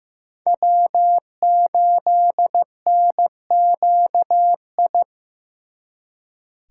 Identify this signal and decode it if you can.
Morse code, heard as a keyed tone, about 18 wpm, 700 Hz. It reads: W8NQI
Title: Speed Racing - USA 1x3 Call Signs 15wpm